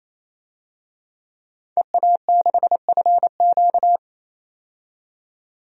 Morse code, heard as a keyed tone, about 27 wpm, 700 Hz. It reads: EA6FQ
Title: DXCC-100 Call Signs - 5 Characters - 28wpm